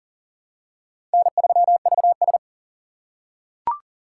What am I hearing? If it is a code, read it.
Morse code, heard as a keyed tone, about 40 wpm, 700 Hz. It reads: N3VS
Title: USA Call Signs - 4 Characters - 3x Character Spacing - 40wpm